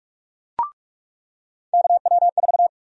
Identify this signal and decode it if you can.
Morse code, heard as a keyed tone, about 46 wpm, 700 Hz. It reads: KW4
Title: USA Call Sign Prefixes - 45wpm